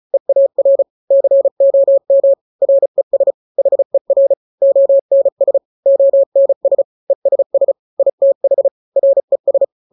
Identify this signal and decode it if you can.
Morse code, heard as a keyed tone, about 34 wpm, 550 Hz. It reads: EAR COM RES HER ONS ONS ESS ITH RES
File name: 3-grams 35 30 WPM